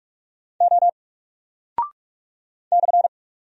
Morse code, heard as a keyed tone, about 44 wpm, 700 Hz. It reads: O /
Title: Single Character (Letter:Number:Punctuation:Prosign) 45wpm